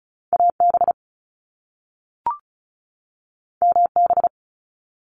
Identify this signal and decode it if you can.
Morse code, heard as a keyed tone, about 36 wpm, 700 Hz. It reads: AB MB
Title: Canadian Provinces and Territories 35wpm